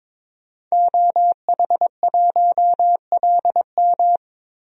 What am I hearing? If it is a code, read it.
Morse code, heard as a keyed tone, about 22 wpm, 700 Hz. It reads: OH1LM
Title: DXCC-100 Call Signs - 5 Characters - Repeated 2x - 22wpm